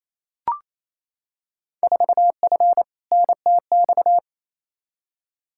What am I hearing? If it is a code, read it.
Morse code, heard as a keyed tone, about 28 wpm, 700 Hz. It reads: 4F NTX
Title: Field Day Exchanges Encore - 28wpm